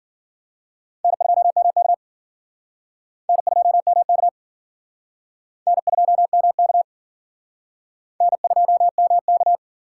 Morse code, heard as a keyed tone, about 50 wpm, 700 Hz. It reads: N2MK N2MK N2MK N2MK